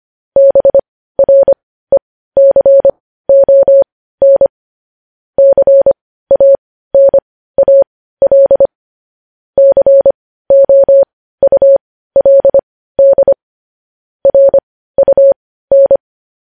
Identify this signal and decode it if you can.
Morse code, heard as a keyed tone, about 25 wpm, spaced as for 15 wpm, 550 Hz. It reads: BRECON CANAL COULD RUN
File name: podcast / News headlines-1x-25x15